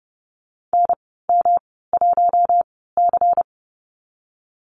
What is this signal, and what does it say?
Morse code, heard as a keyed tone, about 30 wpm, spaced as for 17 wpm, 700 Hz. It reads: NM1C
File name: Lesson 19g - Letter w - Call Signs 30wpm